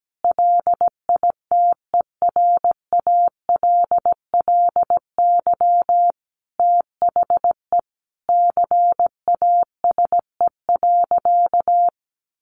Morse code, heard as a keyed tone, about 17 wpm, 700 Hz. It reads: LITERALLY THE CASE.
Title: Sentences from Top 1400 Words 17wpm